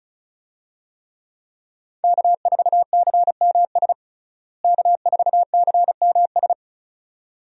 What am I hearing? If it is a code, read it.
Morse code, heard as a keyed tone, about 36 wpm, 700 Hz. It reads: K4CMS K4CMS
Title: USA Call Signs - 5 Characters - Repeated 2x - 35wpm